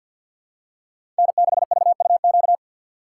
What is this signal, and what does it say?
Morse code, heard as a keyed tone, about 51 wpm, 700 Hz. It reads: N6VUQ
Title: USA Call Signs - 5 Characters - 4x Character Spacing - 50wpm